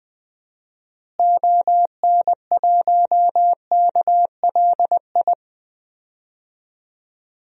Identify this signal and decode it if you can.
Morse code, heard as a keyed tone, about 20 wpm, 700 Hz. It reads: ON1KLI
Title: DXCC-100 Call Signs - 6 Characters - Repeated 3x - 20wpm